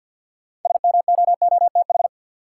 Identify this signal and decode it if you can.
Morse code, heard as a keyed tone, about 53 wpm, 700 Hz. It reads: SMOOTH
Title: Sentences from Top 1400 Words - New Word Drill - 50wpm